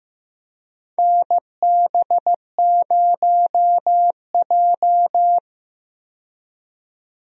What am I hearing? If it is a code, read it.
Morse code, heard as a keyed tone, about 15 wpm, 700 Hz. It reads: NB0J
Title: USA Call Signs - 4 Characters - 15wpm